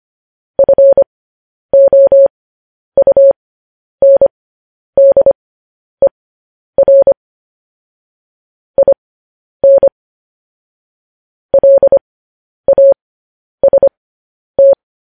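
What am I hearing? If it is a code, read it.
Morse code, heard as a keyed tone, about 25 wpm, spaced as for 10 wpm, 550 Hz. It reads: FOUNDER IN LAST
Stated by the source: Morse code news headlines sent at 10 wpm.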